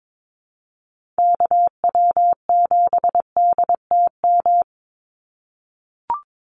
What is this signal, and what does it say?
Morse code, heard as a keyed tone, about 22 wpm, 700 Hz. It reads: KW7DTM
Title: USA Call Signs 22wpm